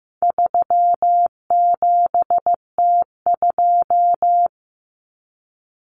Morse code, heard as a keyed tone, about 15 wpm, 700 Hz. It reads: 37T2
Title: Common Cut Numbers - 4-Digit 15wpm